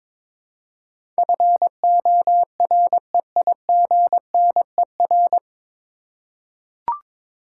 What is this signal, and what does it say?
Morse code, heard as a keyed tone, about 22 wpm, 700 Hz. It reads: FOREIGNER
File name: 9 Letter Words - First 3 Letters Given - 22wpm